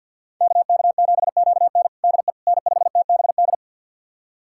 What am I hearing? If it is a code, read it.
Morse code, heard as a keyed tone, about 50 wpm, 700 Hz. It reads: KK7QN DE N5TBD
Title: USA Callsign DE Callsign - 5 Characters - 50wpm